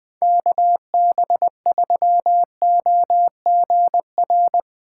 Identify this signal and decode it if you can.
Morse code, heard as a keyed tone, about 20 wpm, 700 Hz. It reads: KB3OGR
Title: Speed Racing - USA 2x3 Call Signs 17wpm